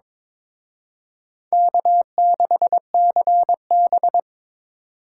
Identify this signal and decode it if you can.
Morse code, heard as a keyed tone, about 26 wpm, 700 Hz. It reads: K6CB
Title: Speed Racing - USA Call Signs 22wpm